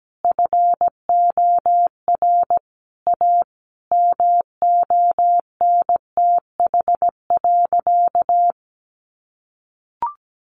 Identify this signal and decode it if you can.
Morse code, heard as a keyed tone, about 17 wpm, 700 Hz. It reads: FOR A MONTH.
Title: Easy Phrases 17wpm